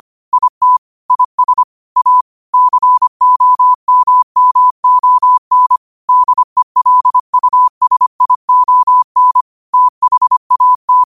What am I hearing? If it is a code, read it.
Morse code, heard as a keyed tone, about 25 wpm, 1000 Hz. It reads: IT IS A COMMON DELUSION THAT
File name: qotdfiles / 25wpm